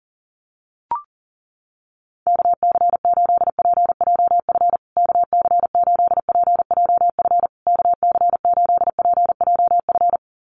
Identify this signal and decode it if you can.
Morse code, heard as a keyed tone, about 40 wpm, 700 Hz. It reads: KC8PJF KC8PJF KC8PJF
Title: USA Call Signs Repeated 3x 40wpm